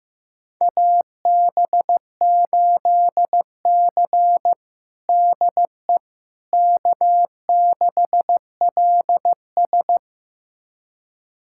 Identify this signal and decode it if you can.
Morse code, heard as a keyed tone, about 15 wpm, 700 Hz. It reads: AB8C DE K6LS
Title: USA Callsign DE Callsign - 4 Characters - 15wpm